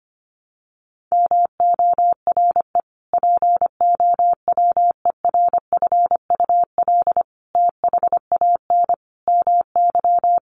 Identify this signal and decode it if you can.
Morse code, heard as a keyed tone, about 25 wpm, 700 Hz. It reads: MORE POWERFUL THAN MY
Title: Sentences from Top 1300 Words 25wpm